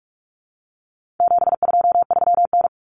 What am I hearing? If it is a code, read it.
Morse code, heard as a keyed tone, about 44 wpm, 700 Hz. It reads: 723N
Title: Common Cut Numbers - 4-Digit 45wpm